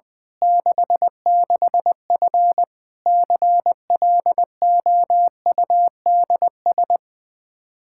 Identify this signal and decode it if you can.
Morse code, heard as a keyed tone, about 20 wpm, 700 Hz. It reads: 66F CLOUDS
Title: Practice QSO - Exchange 2 - Info - 20wpm